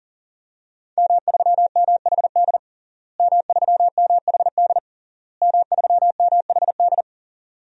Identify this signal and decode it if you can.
Morse code, heard as a keyed tone, about 37 wpm, 700 Hz. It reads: M3MHD M3MHD M3MHD
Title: DXCC-100 Call Signs - 5 Characters - Repeated 3x - 40wpm